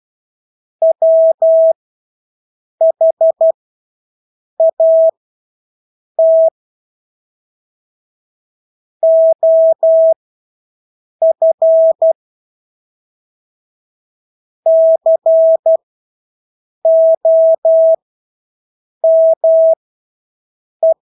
Morse code, heard as a keyed tone, about 12 wpm, spaced as for 6 wpm, 650 Hz. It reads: WHAT OF COME
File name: Engelske ord 12-6 wpm